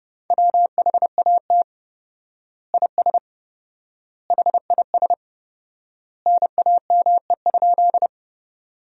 Morse code, heard as a keyed tone, about 30 wpm, 700 Hz. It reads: WHAT IS HIS NAME?
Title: 4x Word-Spacing - Sentences from Top 200 Words - 30wpm